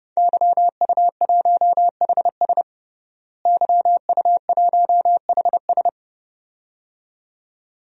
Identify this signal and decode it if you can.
Morse code, heard as a keyed tone, about 32 wpm, 700 Hz. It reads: YU1HS YU1HS
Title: DXCC-100 Call Signs - 5 Characters - Repeated 2x - 30wpm